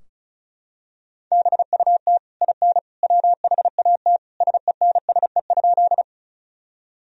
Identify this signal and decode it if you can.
Morse code, heard as a keyed tone, about 34 wpm, 700 Hz. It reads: BUT IN WHAT SENSE?
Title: Sentences from Top 1000 Words - Review Part 4 - 35wpm